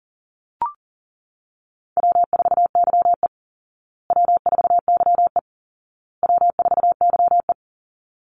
Morse code, heard as a keyed tone, about 40 wpm, 700 Hz. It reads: W4YE W4YE W4YE
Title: Easy Call Signs Repeated 3x - 40wpm